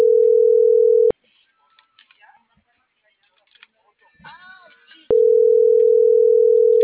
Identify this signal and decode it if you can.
Ringback tone (North America), 440 + 480 Hz.
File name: CENTRAL NMP / out melanie.diaznmp - .45803